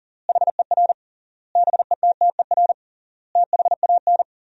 Morse code, heard as a keyed tone, about 40 wpm, 700 Hz. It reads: HER BETTER THAN
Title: 3x Word-Spacing - Sentences from Top 400 Words - 40wpm